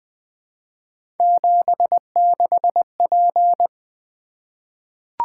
Morse code, heard as a keyed tone, about 20 wpm, 700 Hz. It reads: 76P
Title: Three Letter-Number Combinations - 20wpm